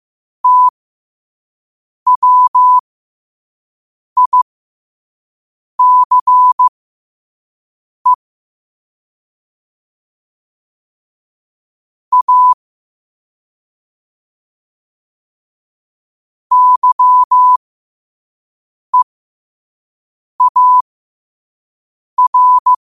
Morse code, heard as a keyed tone, about 15 wpm, spaced as for 5 wpm, 1000 Hz. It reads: TWICE A YEAR
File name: qotdfiles / 5wpm